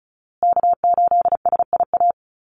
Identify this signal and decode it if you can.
Morse code, heard as a keyed tone, about 34 wpm, 700 Hz. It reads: K8SIA
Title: Call Signs Repeated 2x - 35wpm